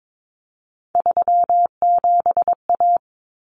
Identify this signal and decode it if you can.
Morse code, heard as a keyed tone, about 22 wpm, 700 Hz. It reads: 37A
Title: Common Cut Numbers - 3-Digit 22wpm